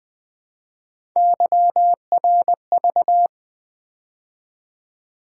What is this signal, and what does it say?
Morse code, heard as a keyed tone, about 24 wpm, 700 Hz. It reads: YRV
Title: Speed-Racing - Three Letter-Number Combinations - 20wpm